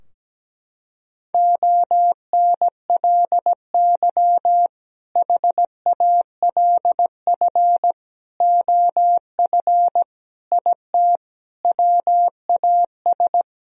Morse code, heard as a keyed tone, about 17 wpm, 700 Hz. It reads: ONLY HALF OF IT WAS
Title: Sentences from Top 1000 Words - Review Part 2 - 17wpm